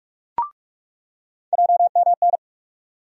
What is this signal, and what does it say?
Morse code, heard as a keyed tone, about 44 wpm, 700 Hz. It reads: JMN
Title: USA Call Sign Postfixes - 3 Letters 45wpm